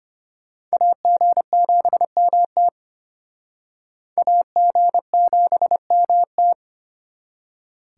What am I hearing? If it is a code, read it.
Morse code, heard as a keyed tone, about 34 wpm, 700 Hz. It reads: AG7MT AG7MT